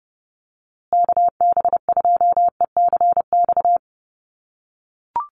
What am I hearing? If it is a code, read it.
Morse code, heard as a keyed tone, about 30 wpm, 700 Hz. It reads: KB2ECX
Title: USA Call Signs Repeated 3x 30wpm